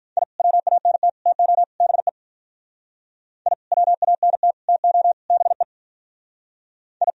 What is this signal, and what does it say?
Morse code, heard as a keyed tone, about 43 wpm, 700 Hz. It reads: I WANT TO BE I WANT TO BE I
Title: Speed-Racing - Sets of 4 Words 35wpm